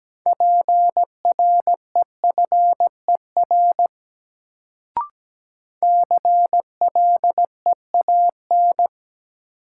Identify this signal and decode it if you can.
Morse code, heard as a keyed tone, about 17 wpm, 700 Hz. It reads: PREFER CLEAN
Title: Sentences from Top 1100 Words - New Word Drill - 17wpm